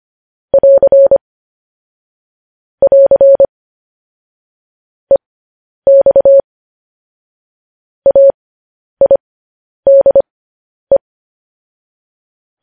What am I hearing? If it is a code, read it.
Morse code, heard as a keyed tone, about 25 wpm, spaced as for 10 wpm, 550 Hz. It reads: <AR> <AR> EX AIDE